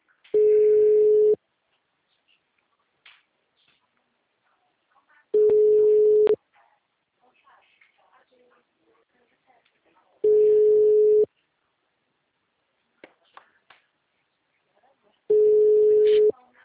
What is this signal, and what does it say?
Ringback tone (Europe, 425 Hz).